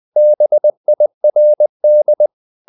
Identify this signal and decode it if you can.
Morse code, heard as a keyed tone, about 20 wpm, 600 Hz. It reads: BIRD
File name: the old man and the sea / the old man and the sea part 1 20wpm